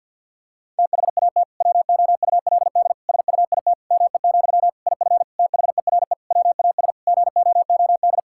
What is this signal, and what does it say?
Morse code, heard as a keyed tone, about 50 wpm, 700 Hz. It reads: THAT WOULD SUIT ME, IF THERE WAS GOOD
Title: Sentences from Top 1600 Words 50wpm